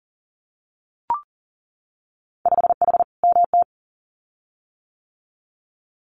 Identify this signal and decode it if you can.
Morse code, heard as a keyed tone, about 40 wpm, 700 Hz. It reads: 5H MT
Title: Winter Field Day Exchanges Encore 40wpm